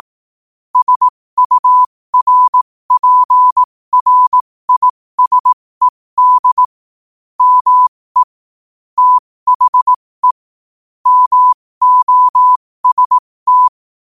Morse code, heard as a keyed tone, about 18 wpm, spaced as for 15 wpm, 1000 Hz. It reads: SURPRISED ME THE MOST
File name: qotdfiles / 15wpm